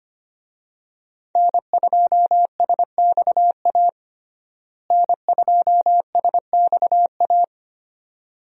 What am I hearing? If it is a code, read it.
Morse code, heard as a keyed tone, about 25 wpm, 700 Hz. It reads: N2SXA N2SXA
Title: USA Call Signs - 5 Characters - Repeated 2x - 25wpm